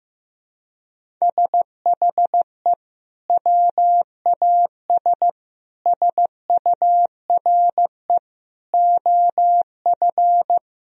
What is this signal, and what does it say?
Morse code, heard as a keyed tone, about 15 wpm, 700 Hz. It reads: SHE WAS SURE OF
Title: Sentences from Top 1000 Words - Review Part 1 - 15wpm